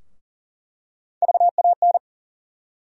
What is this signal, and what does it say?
Morse code, heard as a keyed tone, about 40 wpm, 700 Hz. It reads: VAN
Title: USA Call Sign Postfixes - 3 Letters 40wpm